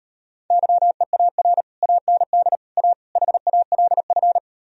Morse code, heard as a keyed tone, about 38 wpm, 700 Hz. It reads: YEAR AND A HALF